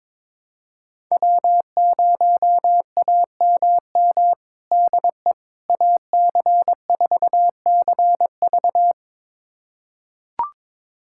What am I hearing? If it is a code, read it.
Morse code, heard as a keyed tone, about 22 wpm, 700 Hz. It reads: W0AMM DE AC4CV
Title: USA Callsign DE Callsign 22wpm